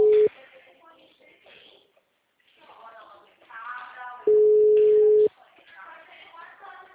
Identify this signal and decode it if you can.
Ringback tone (Europe, 425 Hz), with speech.
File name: out- -2055- - - .6